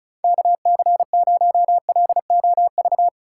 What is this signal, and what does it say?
Morse code, heard as a keyed tone, about 36 wpm, 700 Hz. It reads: KC0LOV
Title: USA Call Signs - 6 Characters - 2x Character Spacing - 35wpm